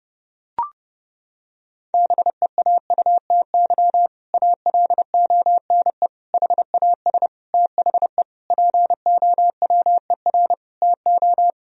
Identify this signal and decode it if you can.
Morse code, heard as a keyed tone, about 30 wpm, 700 Hz. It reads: BEAUTY ALONE HAS THE POWER TO
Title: Sentences from Top 1000 Words - Review Part 2 - 30wpm